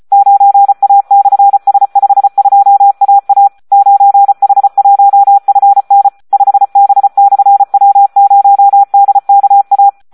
Morse code, heard as a keyed tone, about 34 wpm, 800 Hz. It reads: 9A/S52AA 9H1FN 5B/W0DKA
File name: Audio practica CW / SD 1 34